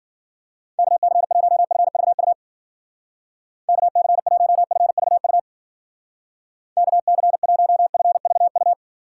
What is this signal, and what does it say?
Morse code, heard as a keyed tone, about 53 wpm, 700 Hz. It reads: KC1FVU KC1FVU KC1FVU